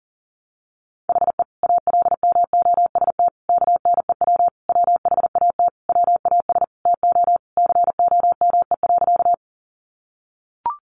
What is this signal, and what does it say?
Morse code, heard as a keyed tone, about 40 wpm, 700 Hz. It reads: HE ALMOST KNEW WHAT WAS TO COME.
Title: Sentences from Top 700 Words 40wpm